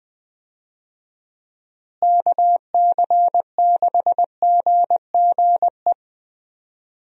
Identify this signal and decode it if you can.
Morse code, heard as a keyed tone, about 20 wpm, 700 Hz. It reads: KC6GGE
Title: USA Call Signs - 6 Characters - Repeated 3x - 20wpm